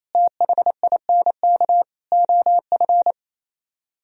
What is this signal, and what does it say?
Morse code, heard as a keyed tone, about 33 wpm, 700 Hz. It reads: THINK OF
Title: Speed-Racing - Sets of 5 Words 28wpm